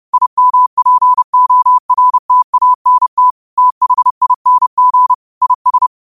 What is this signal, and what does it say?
Morse code, heard as a keyed tone, about 30 wpm, 1000 Hz. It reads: IMPORTANT THING IS